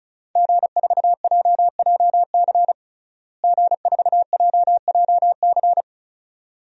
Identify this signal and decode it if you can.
Morse code, heard as a keyed tone, about 34 wpm, 700 Hz. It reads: G4JJC G4JJC
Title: DXCC-100 Call Signs - 5 Characters - Repeated 2x - 35wpm